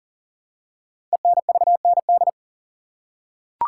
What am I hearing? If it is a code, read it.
Morse code, heard as a keyed tone, about 42 wpm, 700 Hz. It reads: ENVND
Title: Code Groups - 5 Letters - 40wpm